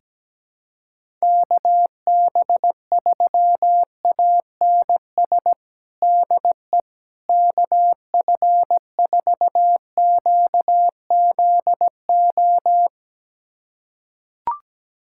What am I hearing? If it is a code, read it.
Morse code, heard as a keyed tone, about 17 wpm, 700 Hz. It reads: KB3ANS DE KF4QZO
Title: USA Callsign DE Callsign - 6 Characters - 17wpm